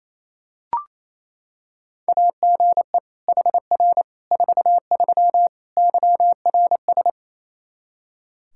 Morse code, heard as a keyed tone, about 28 wpm, 700 Hz. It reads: AGE HR 43 YRS
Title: Practice QSO - Exchange 3 - More Info - 28wpm